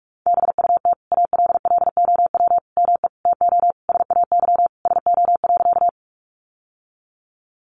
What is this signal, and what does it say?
Morse code, heard as a keyed tone, about 44 wpm, 700 Hz. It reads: BUT ALLOW ME TO SAY SO.
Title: Sentences from Top 700 Words 45wpm